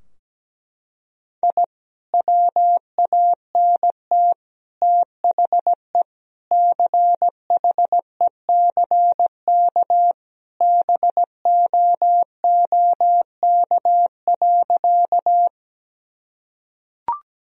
Morse code, heard as a keyed tone, about 17 wpm, 700 Hz. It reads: I WANT THE CHECK BOOK.
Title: Sentences from Top 1000 Words - Review Part 3 - 17wpm